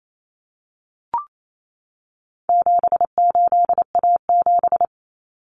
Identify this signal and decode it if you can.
Morse code, heard as a keyed tone, about 28 wpm, 700 Hz. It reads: 78A7
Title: Common Cut Numbers - 4-Digit 28wpm